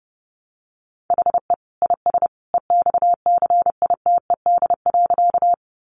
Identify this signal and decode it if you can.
Morse code, heard as a keyed tone, about 30 wpm, 700 Hz. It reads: HE IS EXCITED.
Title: Additional Easy Phrases 30wpm